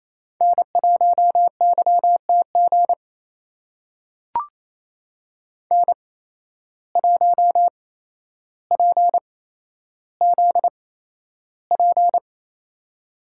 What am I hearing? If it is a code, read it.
Morse code, heard as a keyed tone, about 28 wpm, spaced as for 12 wpm, 700 Hz. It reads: N1YTG N1PZP
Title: USA Call Signs - 5 Characters - 8x Character Spacing - 28wpm